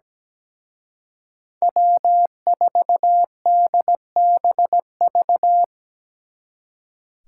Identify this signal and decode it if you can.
Morse code, heard as a keyed tone, about 20 wpm, 700 Hz. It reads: W4DBV
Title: Speed Racing - USA 1x3 Call Signs 17wpm